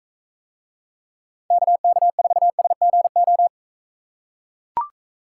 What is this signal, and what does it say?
Morse code, heard as a keyed tone, about 44 wpm, 700 Hz. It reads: KK4SGO